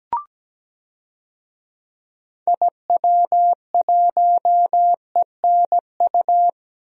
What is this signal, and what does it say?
Morse code, heard as a keyed tone, about 17 wpm, 700 Hz. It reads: IW1ENU
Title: DXCC-100 Call Signs - 6 Characters - 17wpm